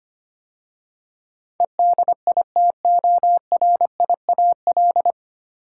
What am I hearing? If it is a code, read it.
Morse code, heard as a keyed tone, about 25 wpm, 700 Hz. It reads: EDITORIAL
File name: practice / 25wpm / 9 Letter Words - First 3 Letters Given - 25wpm